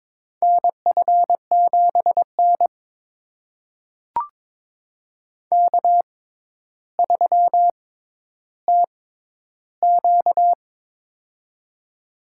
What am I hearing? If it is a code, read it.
Morse code, heard as a keyed tone, about 22 wpm, spaced as for 11 wpm, 700 Hz. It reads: NF7N K3TQ
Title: USA Call Signs - 4 Characters - 6x Character Spacing - 22wpm